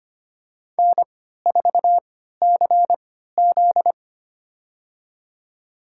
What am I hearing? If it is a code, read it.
Morse code, heard as a keyed tone, about 25 wpm, spaced as for 14 wpm, 700 Hz. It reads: N4CZ
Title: USA Call Signs - 4 Characters - 3x Character Spacing - 25wpm